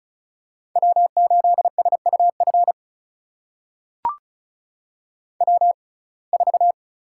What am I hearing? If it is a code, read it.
Morse code, heard as a keyed tone, about 36 wpm, spaced as for 25 wpm, 700 Hz. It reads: W8SUF W4
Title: USA Call Signs - 5 Characters - 6x Character Spacing - 35wpm